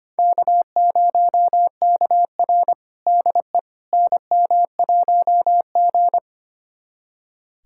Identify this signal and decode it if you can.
Morse code, heard as a keyed tone, about 25 wpm, 700 Hz. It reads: K0KR DE NM1G
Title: USA Callsign DE Callsign - 4 Characters - 25wpm